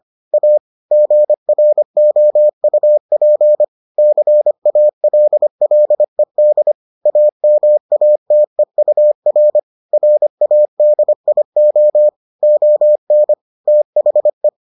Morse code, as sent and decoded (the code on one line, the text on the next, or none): .- / --. .-. --- ..- .--. / -.-. .- .-.. .-.. -.. / .- -- .- - . ..- .-. / .-. .- -.. .. --- / --- -. / - .... .
A GROUP CALLED AMATEUR RADIO ON THE